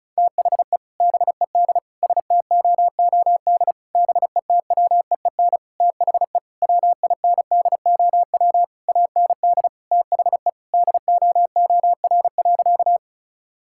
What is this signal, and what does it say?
Morse code, heard as a keyed tone, about 36 wpm, 700 Hz. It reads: THE BED STOOD BETWEEN THE WINDOW AND THE DOOR.
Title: Sentences from Top 1600 Words 35wpm